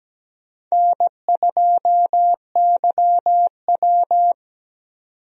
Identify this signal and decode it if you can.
Morse code, heard as a keyed tone, about 17 wpm, 700 Hz. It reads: N2YW
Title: USA Call Signs - 4 Characters - 17wpm